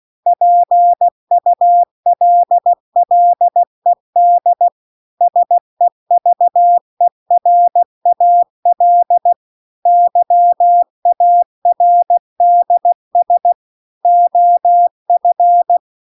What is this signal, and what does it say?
Morse code, heard as a keyed tone, about 16 wpm, 700 Hz. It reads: PULLED SEVERAL YARDS OF
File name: CW-The-Big-Twoheaded-River-2-16WPM